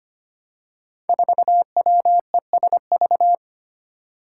Morse code, heard as a keyed tone, about 25 wpm, 700 Hz. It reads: 4WESV
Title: Five Challenging Letter-Number Combinations 25wpm